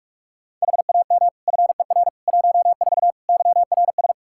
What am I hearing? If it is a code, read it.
Morse code, heard as a keyed tone, about 46 wpm, 700 Hz. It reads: HAM FER 14 YRS
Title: K1USN SST QSO Practice 45wpm